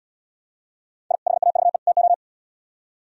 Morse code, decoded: I HAVE TO

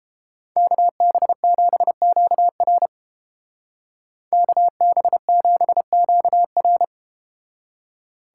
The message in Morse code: KB7QR KB7QR

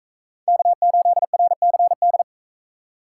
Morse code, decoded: K8RCD

